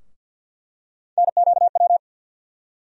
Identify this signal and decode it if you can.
Morse code, heard as a keyed tone, about 52 wpm, 700 Hz. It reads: NQW
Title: USA Call Sign Postfixes - 3 Letters 50wpm